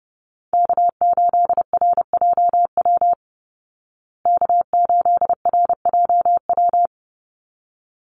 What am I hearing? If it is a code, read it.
Morse code, heard as a keyed tone, about 30 wpm, 700 Hz. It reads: K8RJW K8RJW
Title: Call Signs Repeated 3x - 30wpm